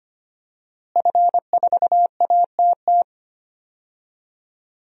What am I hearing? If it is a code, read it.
Morse code, heard as a keyed tone, about 25 wpm, 700 Hz. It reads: F4ATT
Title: DXCC-100 Call Signs - 5 Characters - Repeated 3x - 25wpm